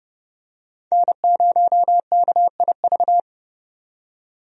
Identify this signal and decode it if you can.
Morse code, heard as a keyed tone, about 30 wpm, 700 Hz. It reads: N0KIV
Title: USA Call Signs - 5 Characters - 30wpm